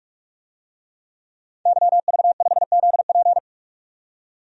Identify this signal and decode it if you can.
Morse code, heard as a keyed tone, about 45 wpm, 700 Hz. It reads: YV5ZP
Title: DXCC-100 Call Signs - 5 Characters - 45wpm